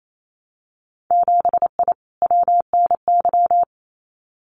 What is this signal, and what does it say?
Morse code, heard as a keyed tone, about 28 wpm, 700 Hz. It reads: 7I WNY
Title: Winter Field Day Exchanges Encore 28wpm